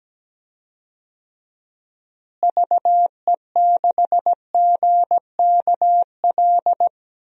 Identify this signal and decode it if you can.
Morse code, heard as a keyed tone, about 17 wpm, 700 Hz. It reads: VE6GKL
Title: DXCC-100 Call Signs - 6 Characters - Repeated 2x - 17wpm